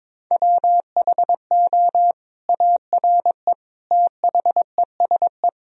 Morse code, heard as a keyed tone, about 22 wpm, 700 Hz. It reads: WHO ARE THESE
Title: Sentences from Top 1000 Words - Review Part 1 - 22wpm